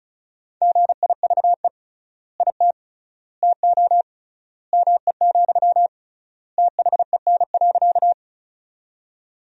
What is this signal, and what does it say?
Morse code, heard as a keyed tone, about 36 wpm, 700 Hz. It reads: GIVE IT TO ME, THEN.
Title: 3x Word-Spacing - Sentences from Top 200 Words - 35wpm